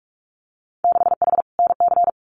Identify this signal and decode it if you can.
Morse code, heard as a keyed tone, about 44 wpm, 690 Hz. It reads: 6H NC
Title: Winter Field Day Exchanges Encore 45wpm